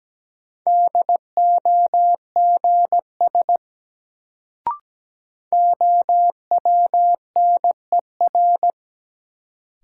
Morse code, heard as a keyed tone, about 17 wpm, 700 Hz. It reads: DOGS OWNER
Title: Sentences from Top 1600 Words - New Word Drill Alt - 17wpm